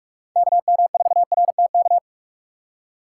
Morse code, heard as a keyed tone, about 46 wpm, 700 Hz. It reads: KM4RTK
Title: USA Call Signs - 6 Characters - 2x Character Spacing - 45wpm